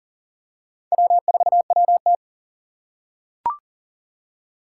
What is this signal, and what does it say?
Morse code, heard as a keyed tone, about 37 wpm, 700 Hz. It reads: W4WT
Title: USA Call Signs - 4 Characters - 2x Character Spacing - 40wpm